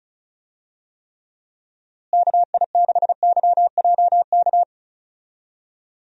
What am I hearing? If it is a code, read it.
Morse code, heard as a keyed tone, about 36 wpm, 700 Hz. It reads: KI6YJK
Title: USA Call Signs - 6 Characters - Repeated 3x - 35wpm